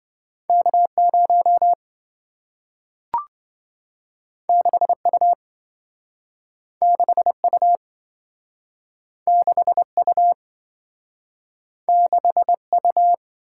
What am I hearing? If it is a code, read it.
Morse code, heard as a keyed tone, about 27 wpm, 700 Hz. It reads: K0 6U 6U 6U 6U